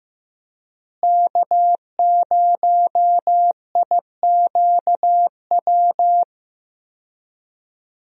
Morse code, heard as a keyed tone, about 15 wpm, 700 Hz. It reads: K0IQW
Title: USA Call Signs - 5 Characters - Repeated 2x - 15wpm